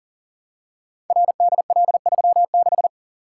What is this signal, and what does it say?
Morse code, heard as a keyed tone, about 40 wpm, 700 Hz. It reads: RDL36
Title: Code Groups - 5 Letter:Numbers - 40wpm